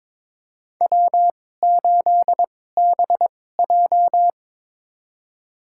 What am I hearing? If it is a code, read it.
Morse code, heard as a keyed tone, about 22 wpm, spaced as for 16 wpm, 700 Hz. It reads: W8BJ
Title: USA Call Signs - 4 Characters - 2x Character Spacing - 22wpm